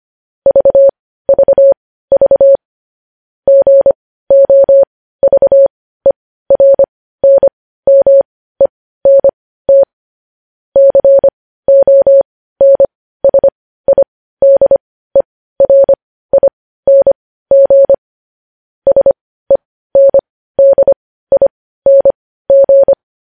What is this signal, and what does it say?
Morse code, heard as a keyed tone, about 25 wpm, spaced as for 15 wpm, 550 Hz. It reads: VVV GOVERNMENT CONSIDERING SENDING
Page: news headlines in morse code at 15 wpm